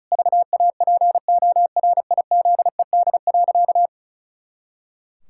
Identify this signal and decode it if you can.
Morse code, heard as a keyed tone, about 34 wpm, 700 Hz. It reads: VAPORIZED.
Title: Pangram Challenge v2 - 35wpm